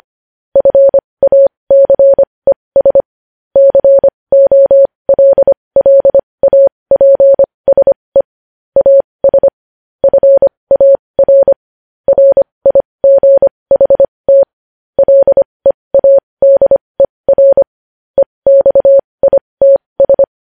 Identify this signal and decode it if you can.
Morse code, heard as a keyed tone, about 25 wpm, spaced as for 20 wpm, 550 Hz. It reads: FACES COLLAPSE AS FAR RIGHT LEADER EXITS
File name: podcast / News headlines-1x-25x20